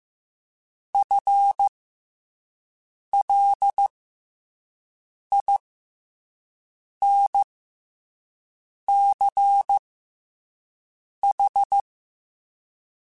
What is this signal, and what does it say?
Morse code, heard as a keyed tone, about 15 wpm, spaced as for 5 wpm, 800 Hz. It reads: FLINCH
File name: Special INGs-mp3s / Special INGs-mp3s / SlowToFaster ing @10